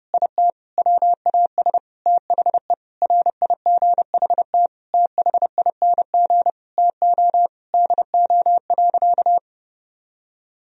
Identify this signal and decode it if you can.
Morse code, heard as a keyed tone, about 32 wpm, 700 Hz. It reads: IT WAS THE RIGHT THING TO DO.
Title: Sentences from Top 1000 Words - Review Part 4 - 30wpm